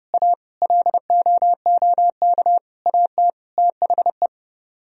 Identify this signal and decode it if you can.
Morse code, heard as a keyed tone, about 29 wpm, 700 Hz. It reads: A LOOK AT THE